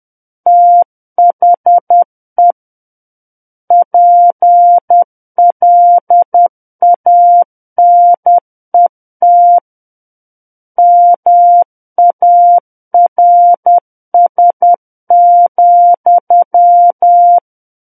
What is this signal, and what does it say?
Morse code, heard as a keyed tone, about 10 wpm, 700 Hz. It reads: THE PLANET MARS,
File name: Book One / 01-Chapter 1 - 10 WPM